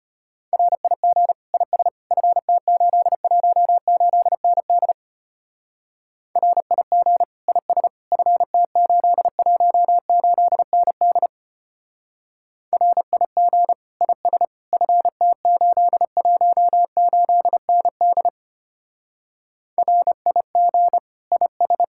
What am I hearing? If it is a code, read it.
Morse code, heard as a keyed tone, about 31 wpm, 700 Hz. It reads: RIG IS FT818ND RIG IS FT818ND RIG IS FT818ND RIG IS
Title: Speed-Racing Common QSO Phrases - 25wpm